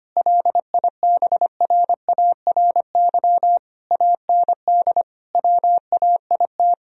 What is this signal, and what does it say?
Morse code, heard as a keyed tone, about 25 wpm, 700 Hz. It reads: LIBRARY AND WAIT
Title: Sentences from Top 1000 Words - Review Part 4 - 25wpm